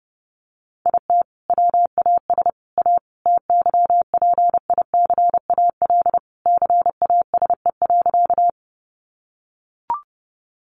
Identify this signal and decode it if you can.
Morse code, heard as a keyed tone, about 28 wpm, 700 Hz. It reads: IT WAS A TYPICAL CASE.
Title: Sentences from Top 1100 Words 30wpm